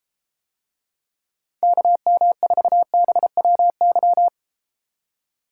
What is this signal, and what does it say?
Morse code, heard as a keyed tone, about 34 wpm, 700 Hz. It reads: KM4BWY